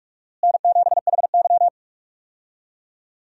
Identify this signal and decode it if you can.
Morse code, heard as a keyed tone, about 42 wpm, 700 Hz. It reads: N7HY
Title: USA Call Signs - 4 Characters - Repeated 2x - 45wpm